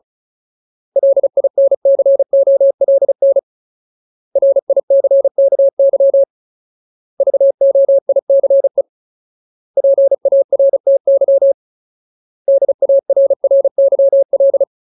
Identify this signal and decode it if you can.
Morse code, heard as a keyed tone, about 34 wpm, 550 Hz. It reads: LINCOLN RICKY VOICE PARTY DARRYL
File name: Ultimate Word List / 35WPM Ultimate